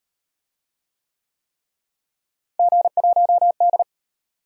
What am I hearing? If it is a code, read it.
Morse code, heard as a keyed tone, about 34 wpm, 700 Hz. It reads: G1D